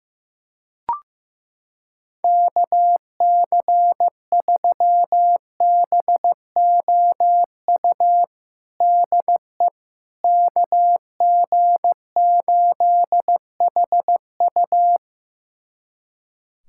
Morse code, heard as a keyed tone, about 16 wpm, 700 Hz. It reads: KC3BOU DE KG8HU
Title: USA Callsign DE Callsign 15wpm